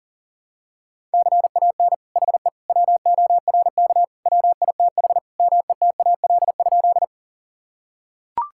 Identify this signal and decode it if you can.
Morse code, heard as a keyed tone, about 37 wpm, 700 Hz. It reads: CAN HE WORK WITH METAL?
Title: Sentences from Top 1000 Words - Review Part 6 - 40wpm